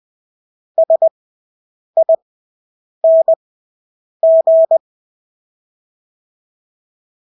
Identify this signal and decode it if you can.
Morse code, heard as a keyed tone, about 20 wpm, spaced as for 8 wpm, 650 Hz. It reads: SING
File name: Engelske ord 20/8 wpm